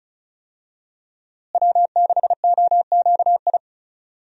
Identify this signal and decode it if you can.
Morse code, heard as a keyed tone, about 36 wpm, 700 Hz. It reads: W6OQI
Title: USA Call Signs - 5 Characters - Repeated 3x - 35wpm